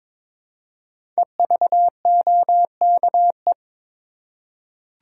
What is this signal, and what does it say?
Morse code, heard as a keyed tone, about 22 wpm, 700 Hz. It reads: EVOKE
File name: 5 Letter Words - First 3 Letters Given - 22wpm